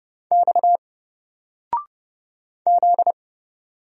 Morse code, heard as a keyed tone, about 30 wpm, 700 Hz. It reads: X Z
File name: Single-Letter-Number-30wpm